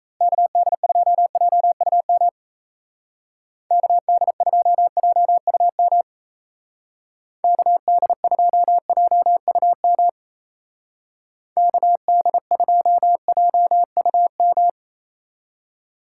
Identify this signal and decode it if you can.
Morse code, heard as a keyed tone, about 36 wpm, 700 Hz. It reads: KD2JUM KD2JUM KD2JUM KD2JUM